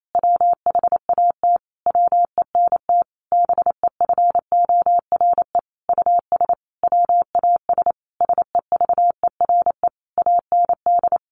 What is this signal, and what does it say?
Morse code, heard as a keyed tone, about 28 wpm, 700 Hz. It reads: WHAT WENT BEFORE US WAS SEVERE AND
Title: Sentences from Top 1400 Words 28wpm